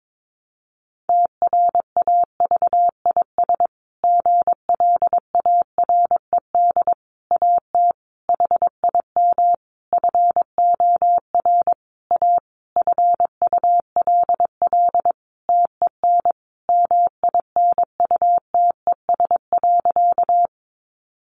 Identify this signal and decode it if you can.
Morse code, heard as a keyed tone, about 22 wpm, 700 Hz. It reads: TRAVIS GLARED AT HIM FOR A FULL TEN MINUTES.
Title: A Sound Of Thunder 22wpm